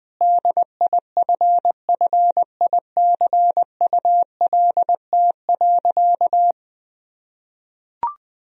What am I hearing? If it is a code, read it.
Morse code, heard as a keyed tone, about 21 wpm, 700 Hz. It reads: DIFFICULT.
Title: Sentences from Top 1000 Words - Review Part 1 - 20wpm